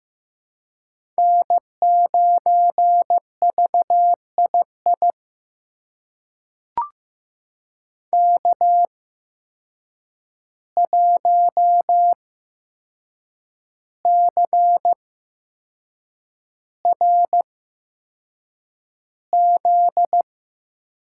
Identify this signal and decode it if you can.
Morse code, heard as a keyed tone, about 15 wpm, spaced as for 6 wpm, 700 Hz. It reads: N9VII K1CRZ
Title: USA Call Signs - 5 Characters - 8x Character Spacing - 15wpm